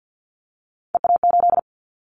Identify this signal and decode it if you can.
Morse code, heard as a keyed tone, about 50 wpm, 690 Hz. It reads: EA8
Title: Callsign Prefixes for Top 100 DXCC - 50wpm